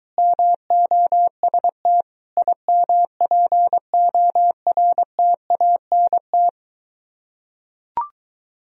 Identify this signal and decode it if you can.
Morse code, heard as a keyed tone, about 23 wpm, 700 Hz. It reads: MOST IMPORTANT